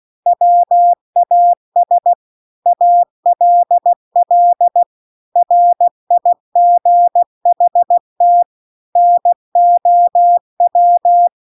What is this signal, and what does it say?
Morse code, heard as a keyed tone, about 16 wpm, 700 Hz. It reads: WAS ALL RIGHT NOW